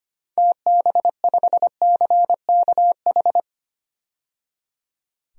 Morse code, heard as a keyed tone, about 25 wpm, 700 Hz. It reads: TB5CKH
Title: DXCC-100 Call Signs - 6 Characters - 25wpm